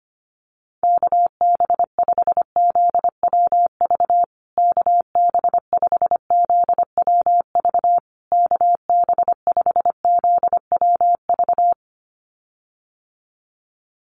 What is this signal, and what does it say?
Morse code, heard as a keyed tone, about 25 wpm, 700 Hz. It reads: KB5ZWV KB5ZWV KB5ZWV
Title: USA Call Signs Repeated 3x 25wpm